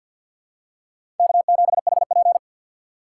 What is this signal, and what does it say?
Morse code, heard as a keyed tone, about 52 wpm, 700 Hz. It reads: K7HP
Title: USA Call Signs - 4 Characters - 2x Character Spacing - 50wpm